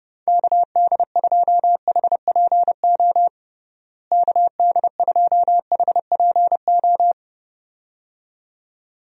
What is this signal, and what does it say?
Morse code, heard as a keyed tone, about 32 wpm, 700 Hz. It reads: KD2HPO KD2HPO
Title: USA Call Signs - 6 Characters - Repeated 2x - 30wpm